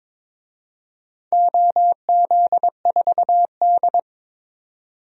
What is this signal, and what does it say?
Morse code, heard as a keyed tone, about 22 wpm, 700 Hz. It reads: OZ4D
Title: DXCC-100 Call Signs - 4 Characters - 22wpm